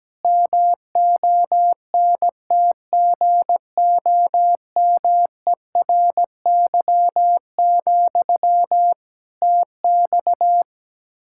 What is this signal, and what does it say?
Morse code, heard as a keyed tone, about 17 wpm, 700 Hz. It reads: MONTGOMERY, TX
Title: K1USN SST QSO Practice 17wpm